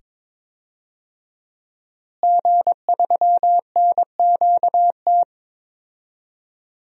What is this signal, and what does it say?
Morse code, heard as a keyed tone, about 22 wpm, 700 Hz. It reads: G3NQT
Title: DXCC-100 Call Signs - 5 Characters - Repeated 2x - 22wpm